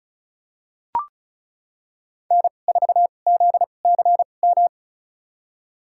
Morse code, heard as a keyed tone, about 36 wpm, spaced as for 25 wpm, 700 Hz. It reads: N4ZCM
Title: USA Call Signs - 5 Characters - 2x Character Spacing - 35wpm